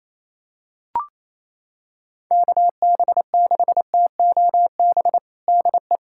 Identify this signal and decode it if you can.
Morse code, heard as a keyed tone, about 28 wpm, 700 Hz. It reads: KB6TOB DE
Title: USA Callsign DE Callsign 28wpm